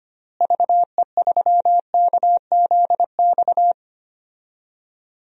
Morse code, heard as a keyed tone, about 25 wpm, 700 Hz. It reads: VE3KZX
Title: DXCC-100 Call Signs - 6 Characters - Repeated 3x - 25wpm